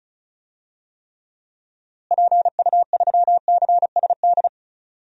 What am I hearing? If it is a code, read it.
Morse code, heard as a keyed tone, about 36 wpm, 700 Hz. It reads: PU3CSD
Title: DXCC-100 Call Signs - 6 Characters - Repeated 2x - 35wpm